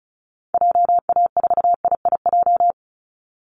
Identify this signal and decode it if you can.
Morse code, heard as a keyed tone, about 34 wpm, 700 Hz. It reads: JA4IIJ
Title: Call Signs Repeated 2x - 35wpm